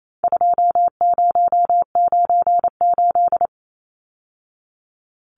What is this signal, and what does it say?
Morse code, heard as a keyed tone, about 28 wpm, 700 Hz. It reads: 2098
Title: Numbers - 4-Digit 28wpm